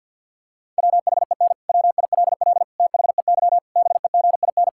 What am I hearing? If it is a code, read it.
Morse code, heard as a keyed tone, about 50 wpm, 700 Hz. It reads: WHEN WILL THEY BEGIN
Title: Sentences from Top 1000 Words - Review Part 6 - 50wpm